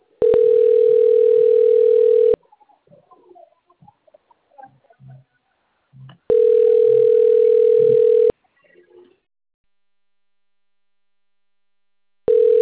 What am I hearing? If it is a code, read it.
Ringback tone (North America), 440 + 480 Hz.